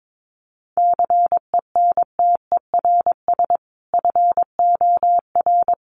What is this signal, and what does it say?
Morse code, heard as a keyed tone, about 22 wpm, 700 Hz. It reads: CENTERS FOR
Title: Sets of 5 Words Encore2 22wpm